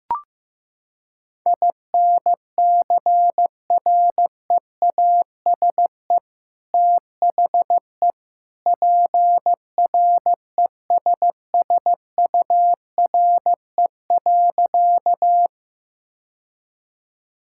Morse code, heard as a keyed tone, about 15 wpm, 700 Hz. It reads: INCREASE THE PRESSURE.
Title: Sentences from Top 1000 Words - Review Part 6 - 15wpm